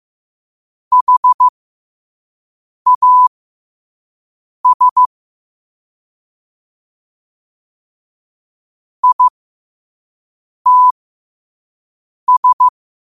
Morse code, heard as a keyed tone, about 15 wpm, spaced as for 5 wpm, 1000 Hz. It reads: HAS ITS
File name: qotdfiles / 5wpm